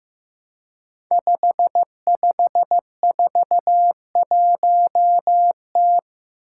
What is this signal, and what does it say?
Morse code, heard as a keyed tone, about 15 wpm, 700 Hz. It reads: 5541T
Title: Five Challenging Letter-Number Combinations 15wpm